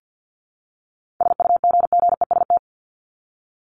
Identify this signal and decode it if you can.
Morse code, heard as a keyed tone, about 50 wpm, 700 Hz. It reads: SUGGEST
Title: ED Suffix - Top 100 Words with Base Words 50wpm